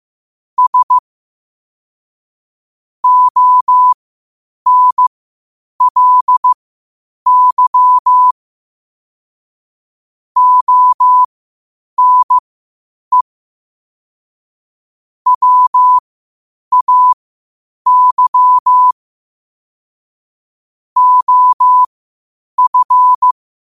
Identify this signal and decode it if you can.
Morse code, heard as a keyed tone, about 15 wpm, spaced as for 8 wpm, 1000 Hz. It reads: S ONLY ONE WAY OF
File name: qotdfiles / 8wpm